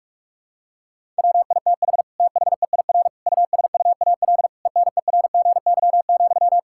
Morse code, heard as a keyed tone, about 41 wpm, 700 Hz. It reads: WITH THEIR USUAL ENERGY,
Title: Sentences from Top 1000 Words - Review Part 6 - 45wpm